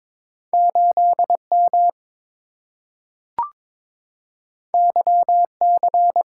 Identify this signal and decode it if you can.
Morse code, heard as a keyed tone, about 22 wpm, 700 Hz. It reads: 8M YC
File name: practice / 22wpm / Two Letter-Number Combinations - 22wpm